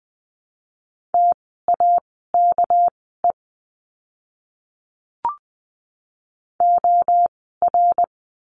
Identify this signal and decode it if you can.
Morse code, heard as a keyed tone, about 20 wpm, spaced as for 15 wpm, 700 Hz. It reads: TAKE OR
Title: 199 Lesson 32f - Letter k - Words 20wpm